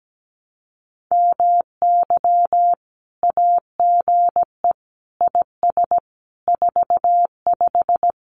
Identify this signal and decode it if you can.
Morse code, heard as a keyed tone, about 17 wpm, 700 Hz. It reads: MY AGE IS 45
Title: Common QSO Phrases Encore 17wpm